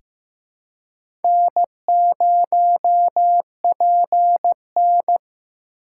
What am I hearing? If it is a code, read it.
Morse code, heard as a keyed tone, about 15 wpm, 700 Hz. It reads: N0PN
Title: USA Call Signs - 4 Characters - 3x Character Spacing - 15wpm